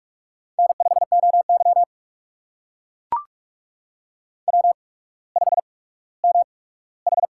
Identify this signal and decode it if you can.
Morse code, heard as a keyed tone, about 44 wpm, spaced as for 19 wpm, 700 Hz. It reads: N5OY W5MH
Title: USA Call Signs - 4 Characters - 8x Character Spacing - 45wpm